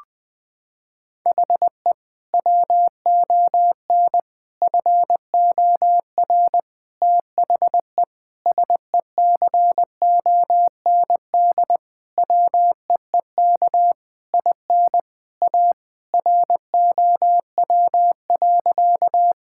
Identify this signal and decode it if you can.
Morse code, heard as a keyed tone, about 20 wpm, 700 Hz. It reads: HE WON FOR THE SECOND WEEK IN A ROW.
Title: Sentences with 7-Word Phrases - 20wpm